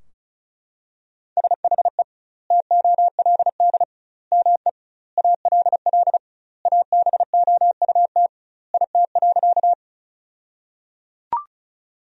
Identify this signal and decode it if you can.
Morse code, heard as a keyed tone, about 36 wpm, 700 Hz. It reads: SHE TOLD ME ALL ABOUT IT.
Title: 2x Word-Spacing - Sentences from Top 400 Words - 35wpm